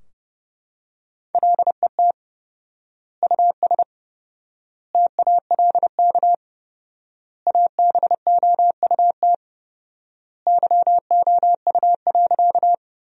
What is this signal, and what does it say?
Morse code, heard as a keyed tone, about 32 wpm, 700 Hz. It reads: LET US TALK ABOUT YOU.
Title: 4x Word-Spacing - Sentences from Top 400 Words - 30wpm